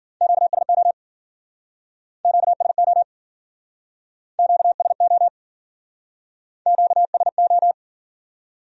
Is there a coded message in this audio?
Yes: Morse code from 0.2 to 7.7 s, heard as a keyed tone, about 50 wpm, 700 Hz. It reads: QSO QSO QSO QSO